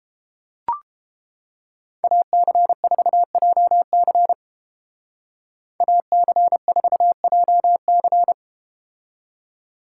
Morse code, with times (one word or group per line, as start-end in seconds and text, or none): 2.0-4.3 AC4JC
5.8-8.3 AC4JC